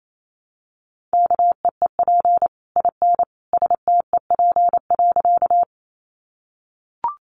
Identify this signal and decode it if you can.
Morse code, heard as a keyed tone, about 28 wpm, 700 Hz. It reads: KEEP IN STEP.
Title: Sentences from Top 400 Words 28wpm